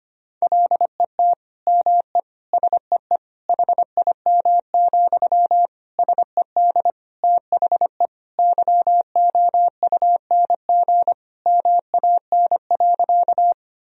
Morse code, heard as a keyed tone, about 25 wpm, 700 Hz. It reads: LET ME SEE HIM, SED THE YOUNG MAN.
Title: Sentences from Top 400 Words and Abbreviations 25wpm